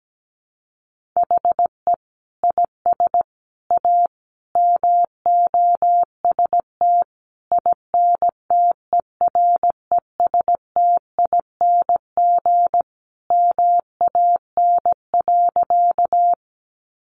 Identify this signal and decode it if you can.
Morse code, heard as a keyed tone, about 17 wpm, 700 Hz. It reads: HE IS A MOST INTERESTING MAN.
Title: Sentences from Top 1000 Words 17wpm